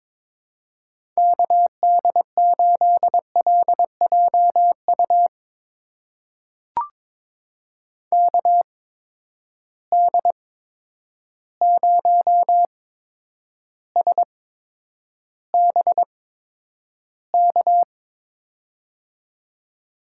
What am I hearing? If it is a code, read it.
Morse code, heard as a keyed tone, about 22 wpm, spaced as for 9 wpm, 700 Hz. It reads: KD8LJU KD0SBK
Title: USA Call Signs - 6 Characters - 8x Character Spacing - 22wpm